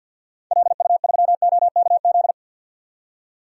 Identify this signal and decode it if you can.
Morse code, heard as a keyed tone, about 50 wpm, 700 Hz. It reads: LU3OKZ
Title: DXCC-100 Call Signs - 6 Characters - 50wpm